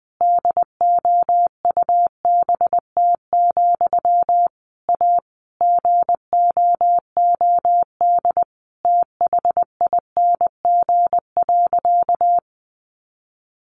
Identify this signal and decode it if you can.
Morse code, heard as a keyed tone, about 20 wpm, 700 Hz. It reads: DOUBT, A GOOD THING.
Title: Sentences from Top 1500 Words 20wpm